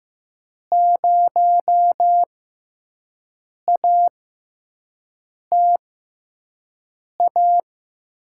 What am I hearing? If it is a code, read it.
Morse code, heard as a keyed tone, about 15 wpm, spaced as for 5 wpm, 700 Hz. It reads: 0ATA
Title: USA Call Signs - 5 Characters - 6x Character Spacing - 15wpm